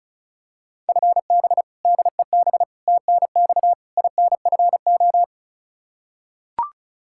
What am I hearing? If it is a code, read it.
Morse code, heard as a keyed tone, about 34 wpm, 700 Hz. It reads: FB DEB TNX INFO
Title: K1USN SST QSO Practice 35wpm